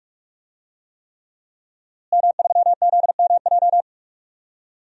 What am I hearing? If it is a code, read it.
Morse code, heard as a keyed tone, about 46 wpm, 700 Hz. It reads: M3ZMJ
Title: DXCC-100 Call Signs - 5 Characters - Repeated 3x - 45wpm